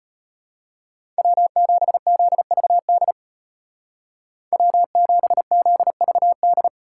Morse code, heard as a keyed tone, about 30 wpm, 700 Hz. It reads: W7ZVD W7ZVD
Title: Speed Racing - USA 1x3 Call Signs 25wpm